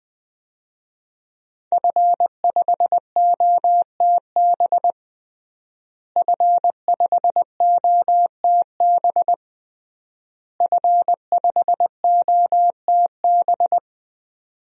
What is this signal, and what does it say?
Morse code, heard as a keyed tone, about 20 wpm, 700 Hz. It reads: F5OTB F5OTB F5OTB
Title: DXCC-100 Call Signs - 5 Characters - Repeated 3x - 20wpm